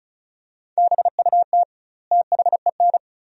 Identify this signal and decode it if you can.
Morse code, heard as a keyed tone, about 34 wpm, 700 Hz. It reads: BUT THEN
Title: 2x Word-Spacing - Sentences from Top 300 Words - 35wpm